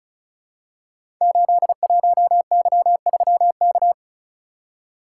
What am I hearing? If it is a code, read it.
Morse code, heard as a keyed tone, about 36 wpm, 700 Hz. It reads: 81Y3K
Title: Code Groups - 5 Letter:Numbers - 35wpm